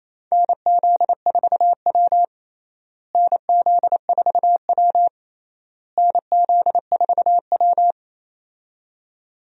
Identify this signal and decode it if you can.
Morse code, heard as a keyed tone, about 28 wpm, 700 Hz. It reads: NZ4W NZ4W NZ4W
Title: USA Call Signs - 4 Characters - Repeated 3x - 28wpm